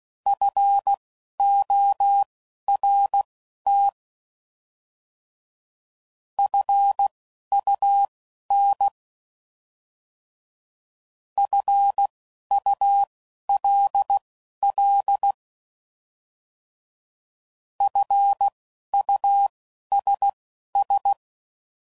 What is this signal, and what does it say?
Morse code, heard as a keyed tone, about 16 wpm, spaced as for 11 wpm, 800 Hz. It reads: FORT FUN FULL FUSS
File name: Prefixes / By First Letter / Fs